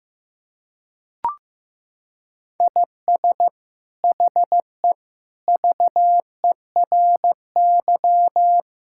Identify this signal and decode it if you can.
Morse code, heard as a keyed tone, about 15 wpm, 700 Hz. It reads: IS HE VERY
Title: Sentences from Top 1000 Words - Review Part 5 - 15wpm